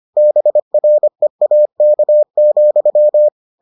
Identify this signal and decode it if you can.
Morse code, heard as a keyed tone, about 25 wpm, 600 Hz. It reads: BREAK,